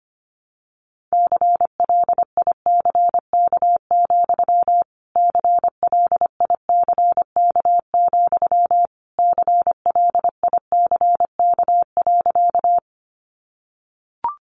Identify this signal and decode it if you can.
Morse code, heard as a keyed tone, about 25 wpm, 700 Hz. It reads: CLICK, CLICK, CLICK.
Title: Sentences from Top 1100 Words 25wpm